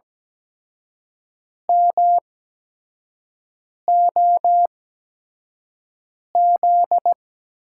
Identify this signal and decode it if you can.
Morse code, heard as a keyed tone, about 17 wpm, spaced as for 5 wpm, 700 Hz. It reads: MOZ
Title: USA Call Signs - 6 Characters - 8x Character Spacing - 17wpm